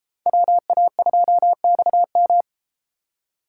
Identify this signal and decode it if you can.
Morse code, heard as a keyed tone, about 38 wpm, 700 Hz. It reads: WA2XM